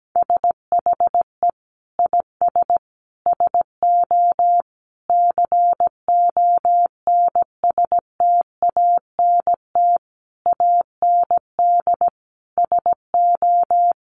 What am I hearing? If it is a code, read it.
Morse code, heard as a keyed tone, about 17 wpm, 700 Hz. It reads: SHE IS SO CONSTANT AND SO
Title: Sentences from Top 1500 Words 17wpm